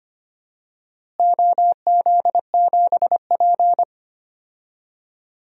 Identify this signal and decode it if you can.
Morse code, heard as a keyed tone, about 25 wpm, 700 Hz. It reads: OZ7P
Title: DXCC-100 Call Signs - 4 Characters - Repeated 2x - 25wpm